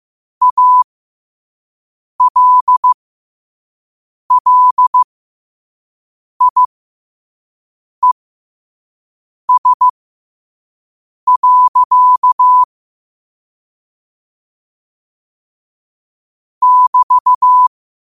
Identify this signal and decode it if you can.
Morse code, heard as a keyed tone, about 15 wpm, spaced as for 5 wpm, 1000 Hz. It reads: ALLIES. <BT>
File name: qotdfiles / 5wpm